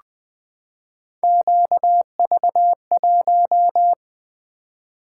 Morse code, heard as a keyed tone, about 20 wpm, 700 Hz. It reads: QV1
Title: Three Letter-Number Combinations - 20wpm